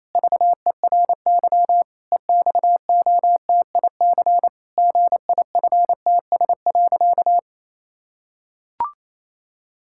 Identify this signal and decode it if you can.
Morse code, heard as a keyed tone, about 28 wpm, 700 Hz. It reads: VERY EXOTIC GIFTS.
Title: Pangram Challenge v2 - 28wpm